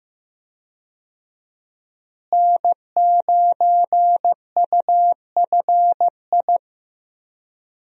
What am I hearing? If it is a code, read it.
Morse code, heard as a keyed tone, about 15 wpm, 700 Hz. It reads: N9UFI
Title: USA Call Signs - 5 Characters - Repeated 3x - 15wpm